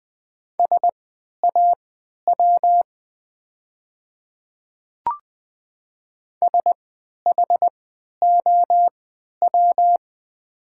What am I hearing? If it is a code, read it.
Morse code, heard as a keyed tone, about 20 wpm, spaced as for 11 wpm, 700 Hz. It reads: SAW SHOW
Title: 3x Character Spacing - Top 300 Words - 20wpm